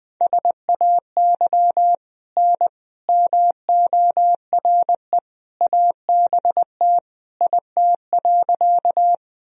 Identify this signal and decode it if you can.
Morse code, heard as a keyed tone, about 21 wpm, 700 Hz. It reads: SAY N MORE ABT IT.
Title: Sentences from Top 200 Words and Abbreviations 20wpm